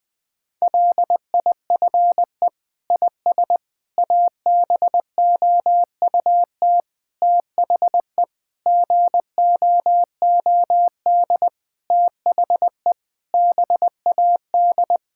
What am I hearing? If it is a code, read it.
Morse code, heard as a keyed tone, about 20 wpm, 700 Hz. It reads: LIFE IS ABOUT THE GOOD THE BAD
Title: Sentences with 6-Word Phrases - 20wpm